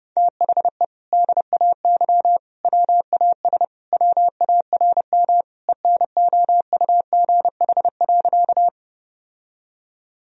Morse code, heard as a keyed tone, about 32 wpm, 700 Hz. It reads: THE DAY WAS WARM ENOUGH.
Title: Sentences from Top 1000 Words - Review Part 2 - 30wpm